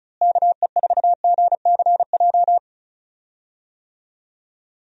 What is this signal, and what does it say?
Morse code, heard as a keyed tone, about 36 wpm, 700 Hz. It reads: KE4GCJ
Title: USA Call Signs - 6 Characters - Repeated 3x - 35wpm